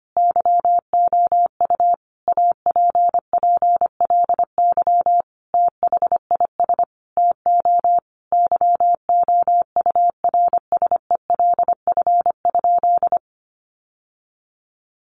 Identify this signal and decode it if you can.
Morse code, heard as a keyed tone, about 25 wpm, 700 Hz. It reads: YOU APPLY THIS TO YOURSELF?
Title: Sentences from Top 1200 Words 25wpm